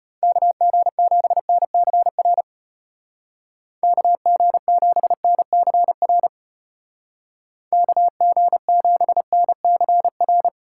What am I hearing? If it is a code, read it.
Morse code, heard as a keyed tone, about 30 wpm, 700 Hz. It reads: KG7NCR KG7NCR KG7NCR
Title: Speed Racing - USA 2x3 Call Signs 25wpm